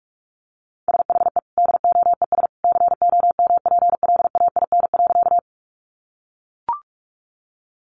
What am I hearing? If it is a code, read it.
Morse code, heard as a keyed tone, about 44 wpm, 700 Hz. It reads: SHE DOES COMPLAIN.
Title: Sentences from Top 1300 Words 45wpm